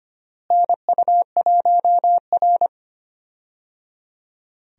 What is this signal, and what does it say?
Morse code, heard as a keyed tone, about 25 wpm, 700 Hz. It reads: NU1R
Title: USA Call Signs - 4 Characters - Repeated 3x - 25wpm